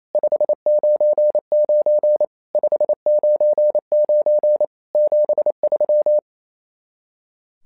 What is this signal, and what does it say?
Morse code, heard as a keyed tone, about 29 wpm, 600 Hz. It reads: 599 599 73
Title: SOTA QSO Practice 28wpm